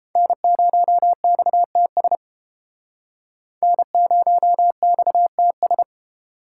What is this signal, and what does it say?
Morse code, heard as a keyed tone, about 31 wpm, 700 Hz. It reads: N0XTS N0XTS